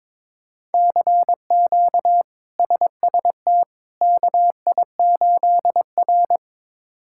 Morse code, heard as a keyed tone, about 22 wpm, 700 Hz. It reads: CQ SST KI8R
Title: K1USN SST QSO Practice - 22wpm